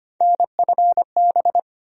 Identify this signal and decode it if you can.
Morse code, heard as a keyed tone, about 25 wpm, 700 Hz. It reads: NFB
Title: USA Call Sign Postfixes - 3 Letters 25wpm